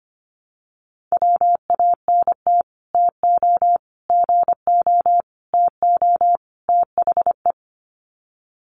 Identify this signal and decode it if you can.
Morse code, heard as a keyed tone, about 25 wpm, 700 Hz. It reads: WANT TO GO TO THE
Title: Sets of 5 Words Encore2 25wpm